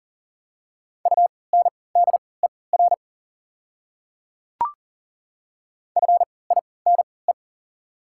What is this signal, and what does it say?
Morse code, heard as a keyed tone, about 40 wpm, spaced as for 23 wpm, 690 Hz. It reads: UNDER FINE
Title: 3x Character Spacing - Top 500 Words - 40wpm